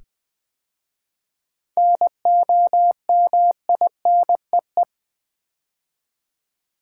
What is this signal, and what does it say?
Morse code, heard as a keyed tone, about 20 wpm, 700 Hz. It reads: NOMINEE
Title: 7 Letter Words - First 3 Letters Given - 20wpm